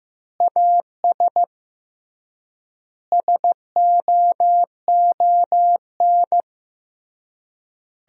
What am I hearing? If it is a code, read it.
Morse code, heard as a keyed tone, about 15 wpm, 700 Hz. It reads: AS SOON
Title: 3x Word-Spacing - Sentences from Top 500 Words - 15wpm